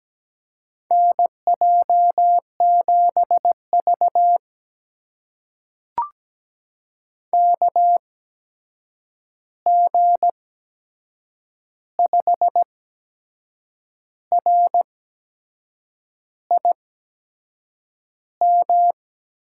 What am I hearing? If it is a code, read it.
Morse code, heard as a keyed tone, about 17 wpm, spaced as for 6 wpm, 700 Hz. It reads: NJ7V KG5RIM
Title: USA Call Signs - 8x Character Spacing - 17wpm